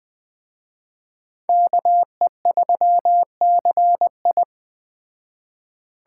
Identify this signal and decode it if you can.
Morse code, heard as a keyed tone, about 20 wpm, 700 Hz. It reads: KE3CI
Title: Speed Racing - USA 2x2 Call Signs 17wpm